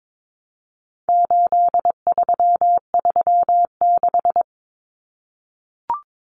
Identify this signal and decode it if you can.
Morse code, heard as a keyed tone, about 22 wpm, 700 Hz. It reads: 8336
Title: Common Cut Numbers - 4-Digit 22wpm